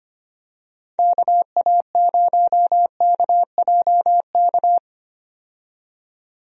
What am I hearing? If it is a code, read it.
Morse code, heard as a keyed tone, about 25 wpm, 700 Hz. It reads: KA0KJK
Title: USA Call Signs - 6 Characters - 25wpm